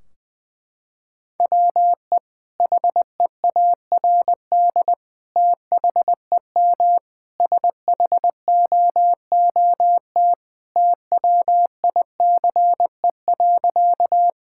Morse code, heard as a keyed tone, about 20 wpm, 700 Hz. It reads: WE HEARD THEM SHOOT TWICE.
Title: Sentences from Top 1600 Words 20wpm